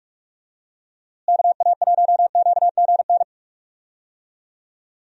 Morse code, heard as a keyed tone, about 44 wpm, 700 Hz. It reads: KA1QGN
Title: USA Call Signs - 6 Characters - Repeated 3x - 45wpm